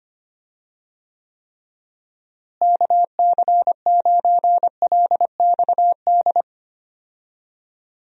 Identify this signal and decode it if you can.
Morse code, heard as a keyed tone, about 25 wpm, 700 Hz. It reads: KC9LXD
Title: USA Call Signs - 6 Characters - Repeated 3x - 25wpm